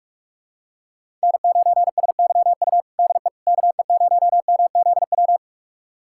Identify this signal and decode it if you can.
Morse code, heard as a keyed tone, about 43 wpm, 700 Hz. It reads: N9SYU DE KE0MZW
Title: USA Callsign DE Callsign 45wpm